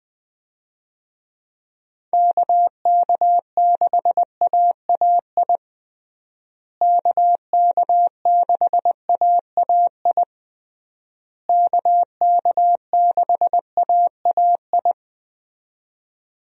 Morse code, heard as a keyed tone, about 20 wpm, 700 Hz. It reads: KK6AAI KK6AAI KK6AAI
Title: USA Call Signs - 6 Characters - Repeated 3x - 20wpm